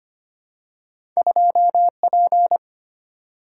Morse code, heard as a keyed tone, about 25 wpm, 700 Hz. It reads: 2P
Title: Single-Letter Single-Number 25wpm